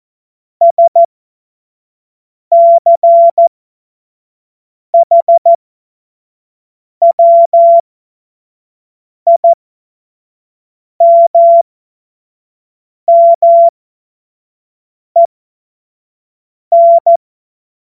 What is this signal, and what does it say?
Morse code, heard as a keyed tone, about 14 wpm, spaced as for 5 wpm, 680 Hz. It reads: SCHWIMMEN
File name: Morsekurs / Witze / Witz 21. Dezember 2021, 25 F70